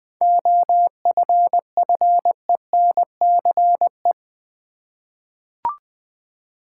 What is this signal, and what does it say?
Morse code, heard as a keyed tone, about 20 wpm, 700 Hz. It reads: OFFENCE
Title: 7 Letter Words - First 3 Letters Given - 20wpm